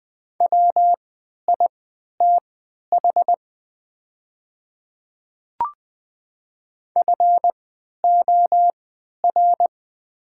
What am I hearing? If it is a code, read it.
Morse code, heard as a keyed tone, about 20 wpm, spaced as for 11 wpm, 700 Hz. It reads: WITH FOR
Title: 3x Character Spacing - Top 100 Words - 20wpm